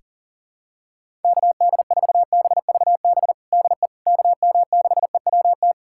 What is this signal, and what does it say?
Morse code, heard as a keyed tone, about 40 wpm, 700 Hz. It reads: KD4BVB DE KM6EWT
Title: USA Callsign DE Callsign - 6 Characters - 40wpm